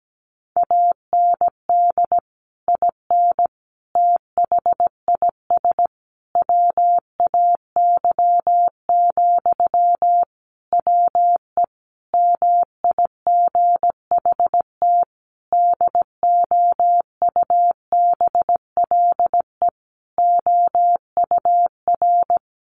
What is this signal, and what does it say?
Morse code, heard as a keyed tone, about 17 wpm, 700 Hz. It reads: AND IN THIS WAY, WE MIGHT DOUBLE OUR
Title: Sentences from Top 1400 Words 17wpm